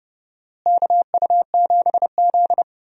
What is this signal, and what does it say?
Morse code, heard as a keyed tone, about 28 wpm, 700 Hz. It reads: KU7Z
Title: USA Call Signs - 4 Characters - Repeated 3x - 30wpm